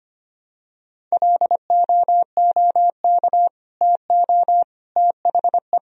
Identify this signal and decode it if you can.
Morse code, heard as a keyed tone, about 25 wpm, 700 Hz. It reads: LOOK TO THE
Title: Sentences from Top 1000 Words - Review Part 1 - 25wpm